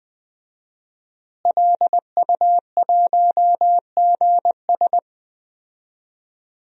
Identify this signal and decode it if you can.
Morse code, heard as a keyed tone, about 20 wpm, 700 Hz. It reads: LU1GS
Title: DXCC-100 Call Signs - 5 Characters - Repeated 3x - 20wpm